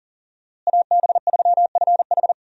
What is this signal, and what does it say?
Morse code, heard as a keyed tone, about 40 wpm, 700 Hz. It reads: AB3FH
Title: USA Call Signs - 5 Characters - Repeated 2x - 40wpm